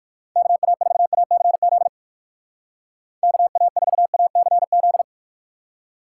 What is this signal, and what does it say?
Morse code, heard as a keyed tone, about 59 wpm, 700 Hz. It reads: KA4ACZ KA4ACZ